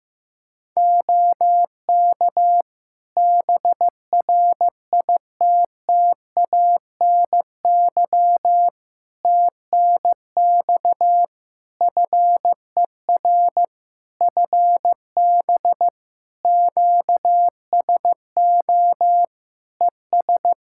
Morse code, heard as a keyed tone, about 15 wpm, 700 Hz. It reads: OK BRITTANY TNX FER FB QSO ES
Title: Practice QSO - Exchange 4 - Ending - 15wpm